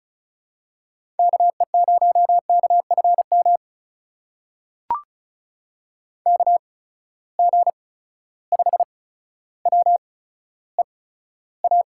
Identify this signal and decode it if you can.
Morse code, heard as a keyed tone, about 36 wpm, spaced as for 15 wpm, 700 Hz. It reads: KE0KFM KG5WEA
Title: USA Call Signs - 6 Characters - 8x Character Spacing - 35wpm